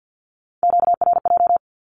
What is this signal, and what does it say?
Morse code, heard as a keyed tone, about 50 wpm, 700 Hz. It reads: QRJ
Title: Q-Codes with Spoken Meaning - 50wpm